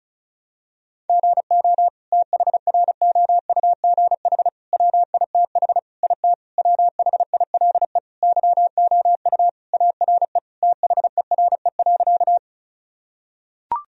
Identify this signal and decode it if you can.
Morse code, heard as a keyed tone, about 34 wpm, 700 Hz. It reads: GO THROUGH WITH IT WHILE YOU ARE THERE.
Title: Sentences from Top 1000 Words - Review Part 1 - 35wpm